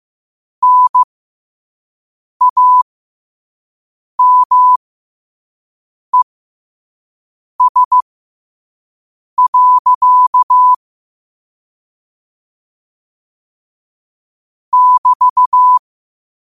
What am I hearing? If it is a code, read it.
Morse code, heard as a keyed tone, about 15 wpm, spaced as for 5 wpm, 1000 Hz. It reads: NAMES. <BT>
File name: qotdfiles / 5wpm